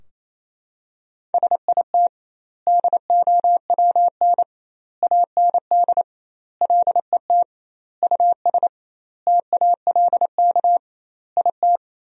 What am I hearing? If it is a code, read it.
Morse code, heard as a keyed tone, about 28 wpm, 700 Hz. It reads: SIT DOWN AND LET US TALK IT
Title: 2x Word-Spacing - Sentences from Top 500 Words - 28wpm